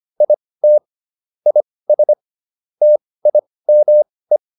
Morse code, heard as a keyed tone, about 25 wpm, spaced as for 18 wpm, 600 Hz. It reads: IT IS TIME